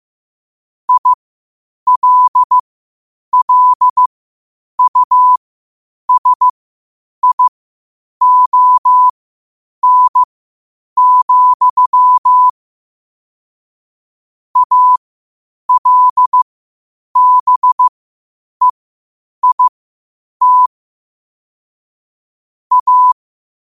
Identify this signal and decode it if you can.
Morse code, heard as a keyed tone, about 15 wpm, spaced as for 8 wpm, 1000 Hz. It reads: ILLUSION, ALBEIT A
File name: qotdfiles / 8wpm